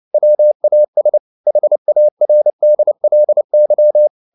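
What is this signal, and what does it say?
Morse code, heard as a keyed tone, about 29 wpm, 600 Hz. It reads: WAS HARDLY
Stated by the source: Morse code practice at 28 wpm, text of Ken Follett's novel Never in English, from CW med Gnister.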